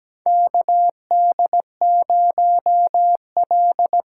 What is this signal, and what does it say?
Morse code, heard as a keyed tone, about 17 wpm, 700 Hz. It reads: KD0L
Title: USA Call Signs - 6 Characters - 8x Character Spacing - 17wpm